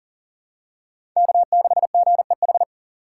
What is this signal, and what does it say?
Morse code, heard as a keyed tone, about 40 wpm, 700 Hz. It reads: K6GEH
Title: USA Call Signs - 5 Characters - 4x Character Spacing - 40wpm